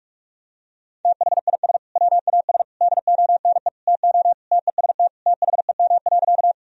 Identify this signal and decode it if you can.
Morse code, heard as a keyed tone, about 42 wpm, 700 Hz. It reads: THIS WAS DONE TO TEST THEM.
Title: Sentences from Top 1000 Words - Review Part 3 - 45wpm